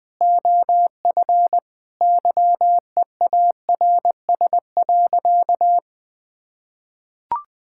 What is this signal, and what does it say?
Morse code, heard as a keyed tone, about 19 wpm, 700 Hz. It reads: OF YEARS.
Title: Sentences with 6-Word Phrases - 20wpm